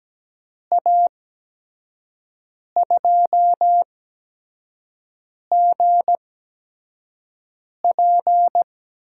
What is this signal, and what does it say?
Morse code, heard as a keyed tone, about 17 wpm, spaced as for 5 wpm, 700 Hz. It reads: A2GP
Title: USA Call Signs - 5 Characters - 8x Character Spacing - 17wpm